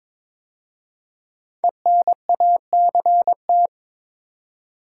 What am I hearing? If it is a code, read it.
Morse code, heard as a keyed tone, about 22 wpm, 700 Hz. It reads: ENACT
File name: 5 Letter Words - First 3 Letters Given - 22wpm